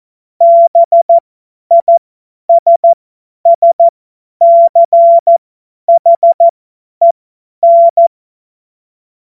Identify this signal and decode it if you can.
Morse code, heard as a keyed tone, about 14 wpm, spaced as for 10 wpm, 680 Hz. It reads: BISSCHEN
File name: Morsekurs / Witze / Witz 11. Mai 2021, 50 F70